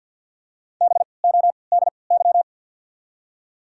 Morse code, heard as a keyed tone, about 54 wpm, spaced as for 29 wpm, 700 Hz. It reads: BODY
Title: 3x Character Spacing - Top 500 Words - 50wpm